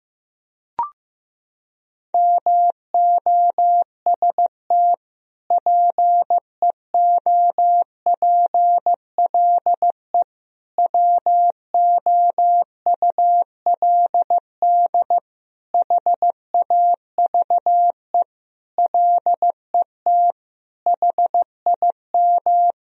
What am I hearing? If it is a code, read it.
Morse code, heard as a keyed tone, about 15 wpm, 700 Hz. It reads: MOST PEOPLE WOULD HAVE LET HIM
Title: Sentences from Top 1000 Words - Review Part 1 - 15wpm